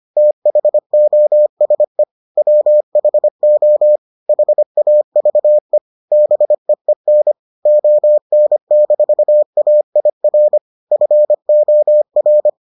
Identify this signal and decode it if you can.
Morse code, heard as a keyed tone, about 25 wpm, 600 Hz. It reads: THOSE WHO HAVE BEEN ON-AIR FOR